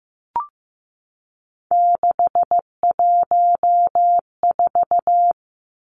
Morse code, heard as a keyed tone, about 15 wpm, 700 Hz. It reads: 614
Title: Three Challenging Letter-Number Combinations 15wpm